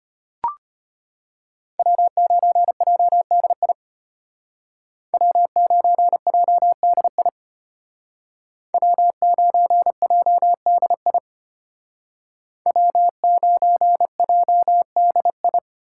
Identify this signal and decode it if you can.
Morse code, heard as a keyed tone, about 32 wpm, 700 Hz. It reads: W9JDI W9JDI W9JDI W9JDI